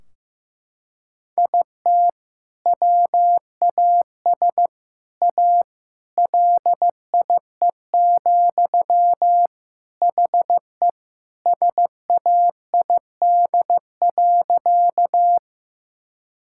Morse code, heard as a keyed tone, about 16 wpm, 700 Hz. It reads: IT WAS A LIE, HE SAID.
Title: Sentences from Top 1600 Words 15wpm